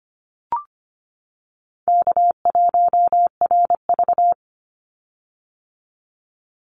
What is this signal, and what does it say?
Morse code, heard as a keyed tone, about 25 wpm, 700 Hz. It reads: K1RV
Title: Easy Call Signs - 25wpm